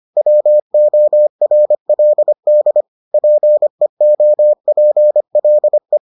Morse code, heard as a keyed tone, about 25 wpm, 600 Hz. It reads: WORLD PEOPLE